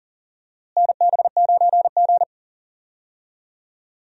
Morse code, heard as a keyed tone, about 39 wpm, 700 Hz. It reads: NB9G
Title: USA Call Signs - 4 Characters - Repeated 3x - 40wpm